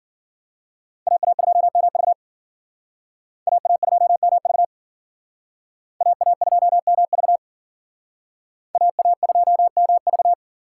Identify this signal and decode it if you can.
Morse code, heard as a keyed tone, about 52 wpm, 700 Hz. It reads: AA2MV AA2MV AA2MV AA2MV